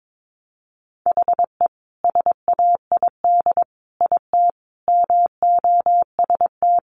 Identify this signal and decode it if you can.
Morse code, heard as a keyed tone, about 22 wpm, 700 Hz. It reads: HE SAID IT MOST
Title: Sentences from Top 1200 Words 22wpm